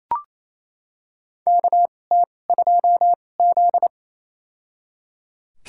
Morse code, heard as a keyed tone, about 28 wpm, spaced as for 20 wpm, 700 Hz. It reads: KT2Z
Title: USA Call Signs - 4 Characters - 2x Character Spacing - 28wpm